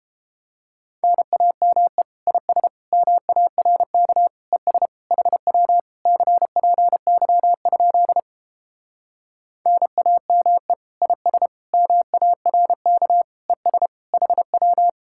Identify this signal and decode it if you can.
Morse code, heard as a keyed tone, about 29 wpm, 700 Hz. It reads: NAME IS MARK ES HW CPY? NAME IS MARK ES HW